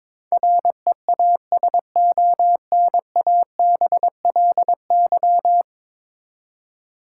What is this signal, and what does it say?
Morse code, heard as a keyed tone, about 22 wpm, 700 Hz. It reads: REASONABLY
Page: morse-code-ninja-practice-22wpm